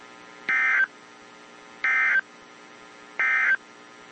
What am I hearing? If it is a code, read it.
Emergency Alert System (SAME) tones, heard as harsh buzzing bursts. It is heard from 0.4 to 3.6 s.
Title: WZ2523 - NOAA Weather Radio